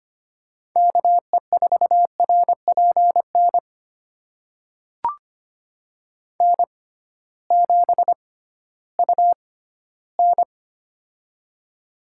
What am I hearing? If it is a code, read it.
Morse code, heard as a keyed tone, about 25 wpm, spaced as for 14 wpm, 700 Hz. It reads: KE4RPN N7UN
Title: USA Call Signs - 6x Character Spacing - 25wpm